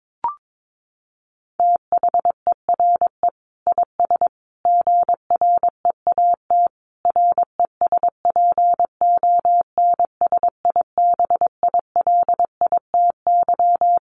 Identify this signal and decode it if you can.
Morse code, heard as a keyed tone, about 22 wpm, 700 Hz. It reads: THERE IS GREAT RESPONSIBILITY
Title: Sentences from Top 1200 Words 22wpm